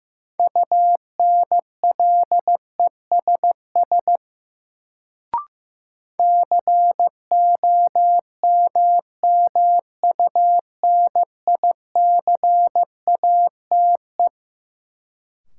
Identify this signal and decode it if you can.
Morse code, heard as a keyed tone, about 15 wpm, 700 Hz. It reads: UNLESS COMMUNICATE
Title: Sentences from Top 1300 Words - New Word Drill Alt - 15wpm